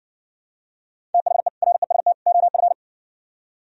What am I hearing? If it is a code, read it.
Morse code, heard as a keyed tone, about 59 wpm, 700 Hz. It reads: THE REST OF